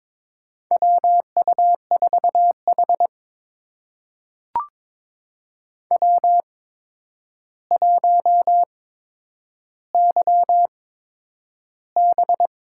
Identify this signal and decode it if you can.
Morse code, heard as a keyed tone, about 22 wpm, spaced as for 9 wpm, 700 Hz. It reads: WU4H W1YB
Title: USA Call Signs - 4 Characters - 8x Character Spacing - 22wpm